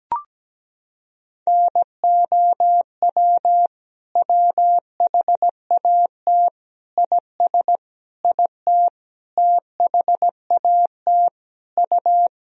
Morse code, heard as a keyed tone, about 17 wpm, 700 Hz. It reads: NOW WHAT IS IT THAT U
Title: Sentences from Top 200 Words and Abbreviations 17wpm